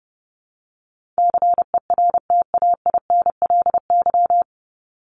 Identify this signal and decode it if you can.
Morse code, heard as a keyed tone, about 28 wpm, 700 Hz. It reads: CERTAINLY
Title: Top 751-1000 Words 30wpm